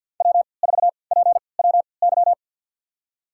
Morse code, heard as a keyed tone, about 52 wpm, spaced as for 29 wpm, 700 Hz. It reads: W4PWY
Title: USA Call Signs - 5 Characters - 3x Character Spacing - 50wpm